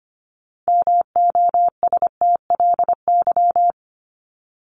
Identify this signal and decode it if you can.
Morse code, heard as a keyed tone, about 25 wpm, 700 Hz. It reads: MOSTLY
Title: Top 1001-1250 Words 25wpm